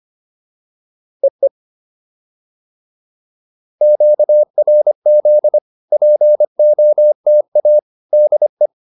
Morse code, heard as a keyed tone, about 25 wpm, 600 Hz. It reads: EE QRZ POTA DE